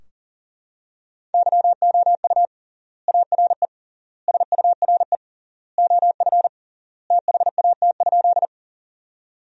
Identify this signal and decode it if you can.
Morse code, heard as a keyed tone, about 37 wpm, 700 Hz. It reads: YOU ARE SURE OF THAT?
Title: 3x Word-Spacing - Sentences from Top 300 Words - 40wpm